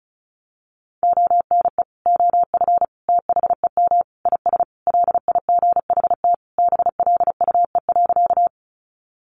Morse code, heard as a keyed tone, about 34 wpm, 700 Hz. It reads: ONE OF THEM IS LIGHT BLUE.
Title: Additional Easy Phrases 35wpm